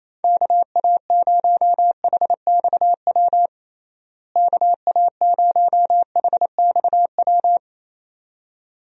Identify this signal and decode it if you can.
Morse code, heard as a keyed tone, about 27 wpm, 700 Hz. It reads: KA0HXW KA0HXW
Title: USA Call Signs - 6 Characters - Repeated 2x - 28wpm